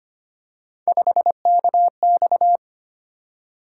Morse code, heard as a keyed tone, about 25 wpm, 700 Hz. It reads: 5KX
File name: Three Letter-Number Combinations - 25wpm